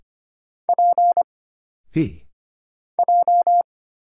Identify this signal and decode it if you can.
Morse code, heard as a keyed tone, about 25 wpm, 700 Hz. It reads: P J
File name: Single Character (Letter:Number:Punctuation:Prosign) - Mind-Melt - 25wpm